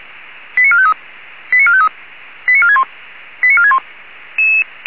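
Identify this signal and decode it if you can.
Five-tone selective call (ZVEI), heard as quick runs of five tones, tones of about 70 ms. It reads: 89453 89453 89461 89461
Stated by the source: ZVEI five-tone call 89456 received on SDR.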